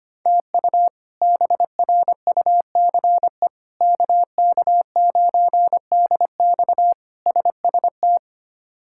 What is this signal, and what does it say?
Morse code, heard as a keyed tone, about 25 wpm, 700 Hz. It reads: TU BRUCE KK9DX SST
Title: K1USN SST QSO Practice - 25wpm